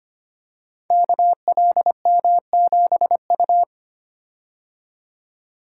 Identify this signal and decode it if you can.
Morse code, heard as a keyed tone, about 25 wpm, 700 Hz. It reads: KLM7U
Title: Code Groups - 5 Letter:Numbers - 25wpm